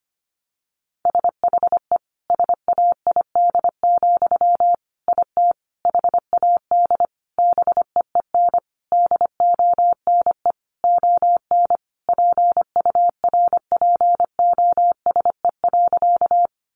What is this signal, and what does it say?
Morse code, heard as a keyed tone, about 25 wpm, 700 Hz. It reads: SHE SAID, IT HAD BEEN DONE ON PURPOSE.
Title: Sentences from Top 800 Words 25wpm